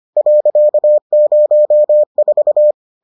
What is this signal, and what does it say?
Morse code, heard as a keyed tone, about 25 wpm, 600 Hz. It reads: .04